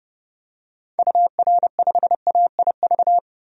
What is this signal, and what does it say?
Morse code, heard as a keyed tone, about 30 wpm, 700 Hz. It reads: UR5AIV
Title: DXCC-100 Call Signs - 6 Characters - 30wpm